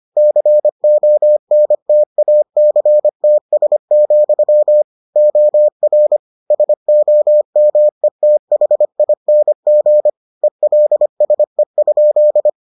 Morse code, heard as a keyed tone, about 25 wpm, 600 Hz. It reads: CONTACTS, OR SOMETHING ELSE?